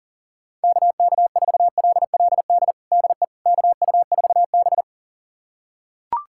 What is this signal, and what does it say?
Morse code, heard as a keyed tone, about 37 wpm, 700 Hz. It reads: KK4LLD DE KU4B
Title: USA Callsign DE Callsign 40wpm